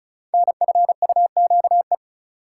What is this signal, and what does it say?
Morse code, heard as a keyed tone, about 35 wpm, 700 Hz. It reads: NFUQE
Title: Code Groups - 5 Letter:Numbers - 35wpm